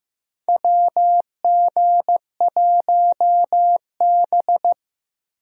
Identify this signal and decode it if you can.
Morse code, heard as a keyed tone, about 15 wpm, 700 Hz. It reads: WG1B
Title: USA Call Signs - 4 Characters - Repeated 2x - 15wpm